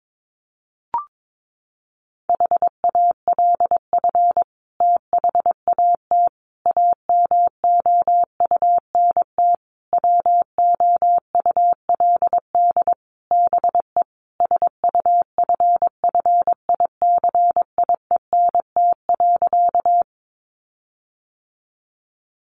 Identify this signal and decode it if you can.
Morse code, heard as a keyed tone, about 22 wpm, 700 Hz. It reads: HALF THAT AMOUNT WOULD BE SUFFICIENT.
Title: Sentences from Top 1400 Words 22wpm